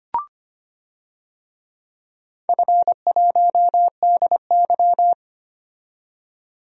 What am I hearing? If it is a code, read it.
Morse code, heard as a keyed tone, about 25 wpm, 700 Hz. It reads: F1DY
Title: DXCC-100 Call Signs - 4 Characters - 25wpm